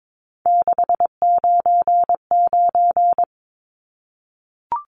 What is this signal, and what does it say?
Morse code, heard as a keyed tone, about 22 wpm, 700 Hz. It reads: 699
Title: Numbers - 3-Digit 22wpm